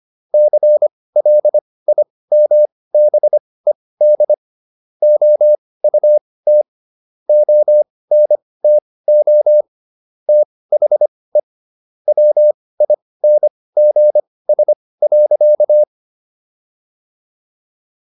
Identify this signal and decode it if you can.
Morse code, heard as a keyed tone, about 25 wpm, spaced as for 18 wpm, 600 Hz. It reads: CLIMBED OUT ONTO THE WINGS.